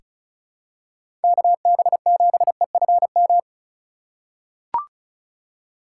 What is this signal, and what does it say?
Morse code, heard as a keyed tone, about 36 wpm, 700 Hz. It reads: KB7EFM
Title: USA Call Signs - 6 Characters - 4x Character Spacing - 35wpm